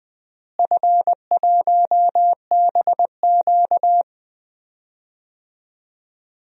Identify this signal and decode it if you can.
Morse code, heard as a keyed tone, about 20 wpm, 700 Hz. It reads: F1BQ
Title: DXCC-100 Call Signs - 4 Characters - Repeated 2x - 20wpm